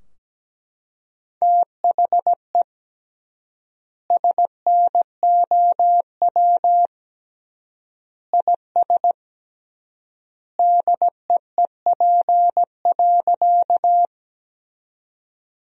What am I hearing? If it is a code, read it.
Morse code, heard as a keyed tone, about 17 wpm, 700 Hz. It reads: THE SNOW IS DEEP.
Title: 3x Word-Spacing - Sentences from Top 500 Words - 17wpm